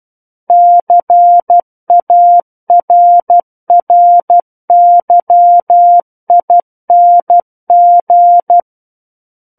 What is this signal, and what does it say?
Morse code, heard as a keyed tone, about 12 wpm, 700 Hz. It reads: CARRYING
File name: Book One / 03-Chapter 3 - 12 WPM